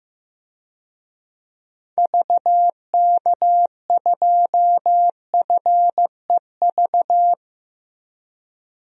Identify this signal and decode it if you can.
Morse code, heard as a keyed tone, about 15 wpm, 700 Hz. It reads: VK2FEV
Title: DXCC-100 Call Signs - 6 Characters - Repeated 2x - 15wpm